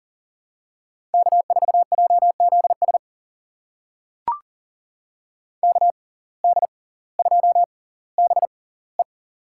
Morse code, heard as a keyed tone, about 39 wpm, spaced as for 20 wpm, 700 Hz. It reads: K4JZS KD2BE
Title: USA Call Signs - 5 Characters - 6x Character Spacing - 40wpm